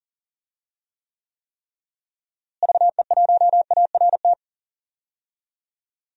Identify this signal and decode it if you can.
Morse code, heard as a keyed tone, about 40 wpm, 700 Hz. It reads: VE1ART
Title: DXCC-100 Call Signs - 6 Characters - Repeated 3x - 40wpm